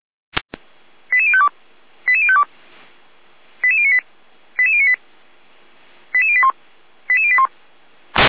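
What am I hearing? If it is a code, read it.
Five-tone selective call (ZVEI), heard as quick runs of five tones, tones of about 70 ms. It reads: 88052 88052 88098 88098 88081 88081
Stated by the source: Received on SDR as a ZVEI five-tone selective call.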